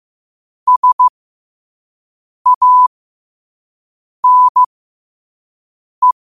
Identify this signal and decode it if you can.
Morse code, heard as a keyed tone, about 15 wpm, spaced as for 5 wpm, 1000 Hz. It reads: SANE
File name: qotdfiles / 5wpm